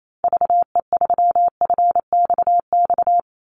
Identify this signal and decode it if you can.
Morse code, heard as a keyed tone, about 28 wpm, 700 Hz. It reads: VE3FXX
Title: Call Signs Repeated 2x - 28wpm